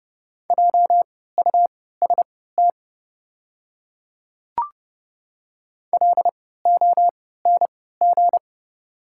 Morse code, heard as a keyed tone, about 30 wpm, spaced as for 17 wpm, 700 Hz. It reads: JUST LONG
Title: 3x Character Spacing - Top 200 Words - 30wpm